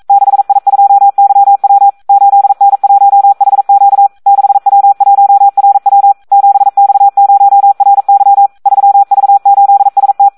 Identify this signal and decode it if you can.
Morse code, heard as a keyed tone, about 42 wpm, 800 Hz. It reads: 6I2YW 8N1HQ 6W1RW 7X0RY 3V8ST
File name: Audio practica CW / SD 1 42